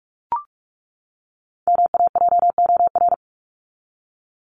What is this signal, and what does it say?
Morse code, heard as a keyed tone, about 44 wpm, 700 Hz. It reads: MAJOR
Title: Top 501-750 Words 45wpm